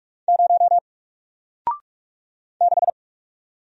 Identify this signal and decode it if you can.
Morse code, heard as a keyed tone, about 46 wpm, 700 Hz. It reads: 0 6
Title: Single Letter-Number Alt 45wpm